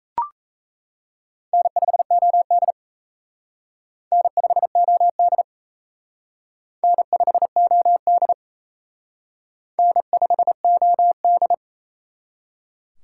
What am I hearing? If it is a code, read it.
Morse code, heard as a keyed tone, about 38 wpm, 700 Hz. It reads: N5OD N5OD N5OD N5OD